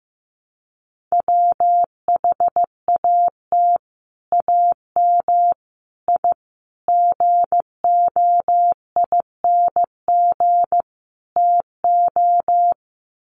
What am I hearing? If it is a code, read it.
Morse code, heard as a keyed tone, about 15 wpm, 700 Hz. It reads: WHAT AM I GOING TO
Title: Sets of 5 Words Encore 15wpm